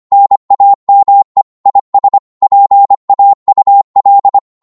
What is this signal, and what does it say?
Morse code, heard as a keyed tone, about 25 wpm, 800 Hz. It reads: NAME IS PAUL
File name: Morse / qso-25wpm